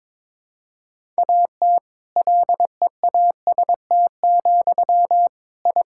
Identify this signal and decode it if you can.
Morse code, heard as a keyed tone, about 21 wpm, 700 Hz. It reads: AT LEAST, I
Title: Sentences from Top 1000 Words - Review Part 5 - 22wpm